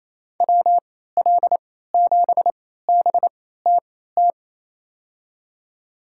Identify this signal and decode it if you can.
Morse code, heard as a keyed tone, about 28 wpm, spaced as for 16 wpm, 700 Hz. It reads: WL7BTT
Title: USA Call Signs - 3x Character Spacing - 28wpm